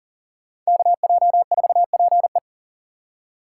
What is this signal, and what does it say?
Morse code, heard as a keyed tone, about 40 wpm, 700 Hz. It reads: KJ4PE
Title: USA Call Signs - 3x Character Spacing - 40wpm